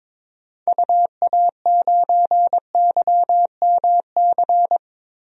Morse code, heard as a keyed tone, about 22 wpm, 700 Hz. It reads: UA9YMC
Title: DXCC-100 Call Signs - 6 Characters - Repeated 2x - 22wpm